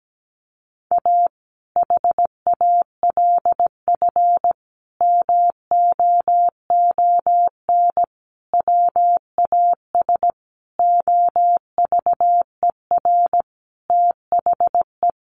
Morse code, heard as a keyed tone, about 17 wpm, 700 Hz. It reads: A HALF MOON WAS OVER THE
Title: Sentences from Top 1100 Words 17wpm